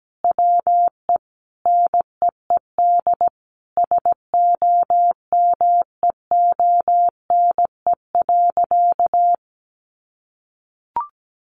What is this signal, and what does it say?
Morse code, heard as a keyed tone, about 17 wpm, 700 Hz. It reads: WE NEED SOMEONE.
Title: Sentences from Top 1300 Words 17wpm